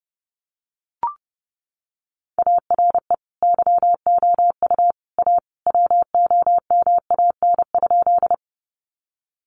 Morse code, heard as a keyed tone, about 30 wpm, 700 Hz. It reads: ARE YOU A WOMAN?
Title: Sentences from Top 1100 Words 30wpm